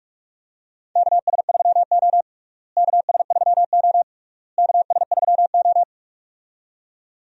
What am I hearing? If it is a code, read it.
Morse code, heard as a keyed tone, about 41 wpm, 700 Hz. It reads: KS3O KS3O KS3O
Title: USA Call Signs - 4 Characters - Repeated 3x - 45wpm